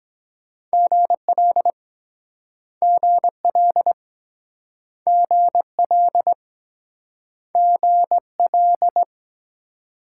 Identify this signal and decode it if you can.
Morse code, heard as a keyed tone, about 22 wpm, 700 Hz. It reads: GL GL GL GL